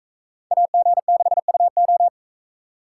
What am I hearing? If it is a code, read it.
Morse code, heard as a keyed tone, about 41 wpm, 700 Hz. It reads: AG6UO